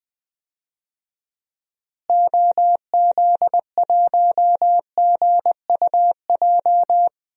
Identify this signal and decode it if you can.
Morse code, heard as a keyed tone, about 20 wpm, 700 Hz. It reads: OZ1GUJ
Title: DXCC-100 Call Signs - 6 Characters - Repeated 3x - 20wpm